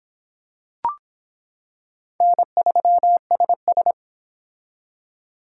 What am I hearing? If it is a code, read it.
Morse code, heard as a keyed tone, about 25 wpm, 700 Hz. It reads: N3SS